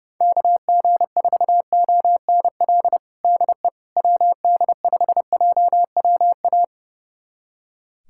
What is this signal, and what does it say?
Morse code, heard as a keyed tone, about 32 wpm, 700 Hz. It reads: KG4ONL DE WD5JWA
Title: USA Callsign DE Callsign - 6 Characters - 30wpm